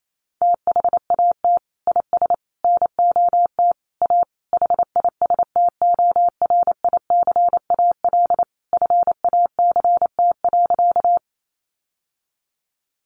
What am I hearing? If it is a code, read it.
Morse code, heard as a keyed tone, about 28 wpm, 700 Hz. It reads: THAT IS NOT A HISTORICAL FACT.
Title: Sentences from Top 600 Words 28wpm